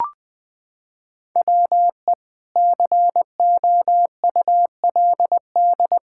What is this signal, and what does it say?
Morse code, heard as a keyed tone, about 21 wpm, 700 Hz. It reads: WE COULD
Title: Pangram Challenge v2 - 20wpm